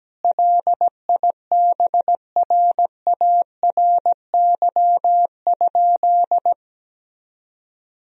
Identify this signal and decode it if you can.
Morse code, heard as a keyed tone, about 17 wpm, 700 Hz. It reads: LIBRARY?
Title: Sentences from Top 1000 Words - Review Part 4 - 17wpm